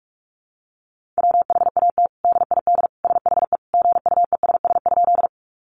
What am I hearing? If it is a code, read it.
Morse code, heard as a keyed tone, about 44 wpm, 700 Hz. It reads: WHAT DID SHE GUESS?
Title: Sentences from Top 1500 Words 45wpm